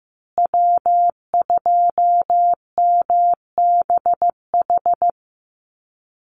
Morse code, heard as a keyed tone, about 15 wpm, 700 Hz. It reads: W2MBH
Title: USA Call Signs Repeated 2x 15wpm